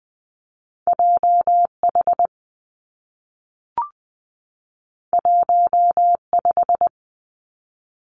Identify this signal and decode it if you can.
Morse code, heard as a keyed tone, about 20 wpm, 700 Hz. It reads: J H 1 5
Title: Two Challenging Letter-Number Combinations 20wpm